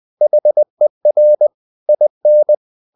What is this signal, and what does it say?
Morse code, heard as a keyed tone, about 20 wpm, 600 Hz. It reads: HER IN
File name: the old man and the sea / the old man and the sea part 1 20wpm